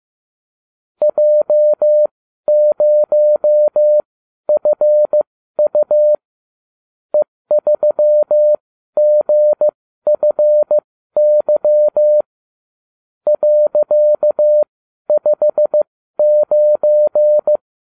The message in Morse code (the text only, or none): J0FU E3GFY .59